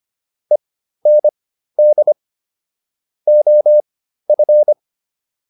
Morse code, heard as a keyed tone, about 25 wpm, spaced as for 13 wpm, 600 Hz. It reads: END OF